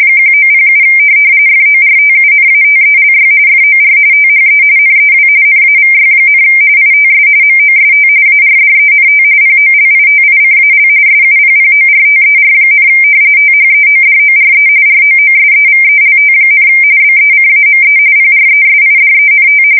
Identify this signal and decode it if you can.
RTTY, about 45.45 baud. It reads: 1907 FORTUNATELY FOR THE NEW NAVY, ITS MOST ARDENT POLITICAL SUPPORTER, THEODORE ROOSEVELT, BECAME PRESIDENT
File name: fsk170 / usnavy3